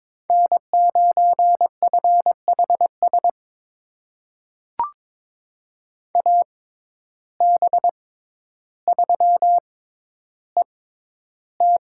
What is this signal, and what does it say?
Morse code, heard as a keyed tone, about 22 wpm, spaced as for 11 wpm, 700 Hz. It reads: N9FHS AB3ET
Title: USA Call Signs - 5 Characters - 6x Character Spacing - 22wpm